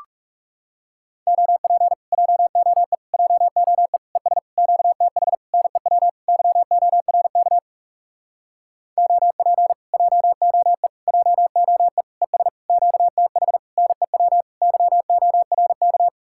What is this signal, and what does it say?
Morse code, heard as a keyed tone, about 40 wpm, 700 Hz. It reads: OP JOE JOE ES QTH NEW YORK OP JOE JOE ES QTH NEW YORK